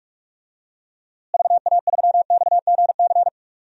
Morse code, heard as a keyed tone, about 42 wpm, 700 Hz. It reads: VA3XGC
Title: DXCC-100 Call Signs - 6 Characters - Repeated 3x - 45wpm